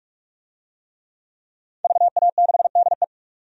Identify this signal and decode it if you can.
Morse code, heard as a keyed tone, about 46 wpm, 700 Hz. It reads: VA6DE
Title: DXCC-100 Call Signs - 5 Characters - 45wpm